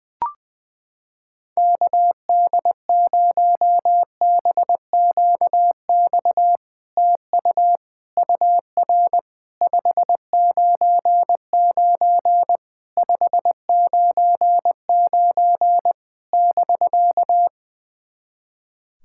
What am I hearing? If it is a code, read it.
Morse code, heard as a keyed tone, about 21 wpm, 700 Hz. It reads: KD0BQX TU UR 599 599 <BK>
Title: SOTA QSO Practice 20wpm